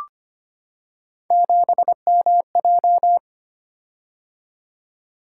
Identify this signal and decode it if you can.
Morse code, heard as a keyed tone, about 25 wpm, 700 Hz. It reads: 7MJ
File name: Three Letter-Number Combinations - 25wpm